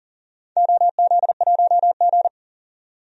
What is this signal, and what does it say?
Morse code, heard as a keyed tone, about 40 wpm, 700 Hz. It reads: OZ1G
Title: DXCC-100 Call Signs - 4 Characters - 40wpm